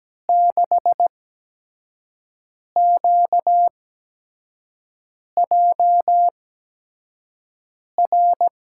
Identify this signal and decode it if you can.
Morse code, heard as a keyed tone, about 17 wpm, spaced as for 5 wpm, 700 Hz. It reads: 6QJR
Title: USA Call Signs - 6 Characters - 8x Character Spacing - 17wpm